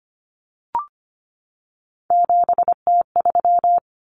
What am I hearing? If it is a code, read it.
Morse code, heard as a keyed tone, about 25 wpm, 700 Hz. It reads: 7T3
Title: Common Cut Numbers - 3-Digit 25wpm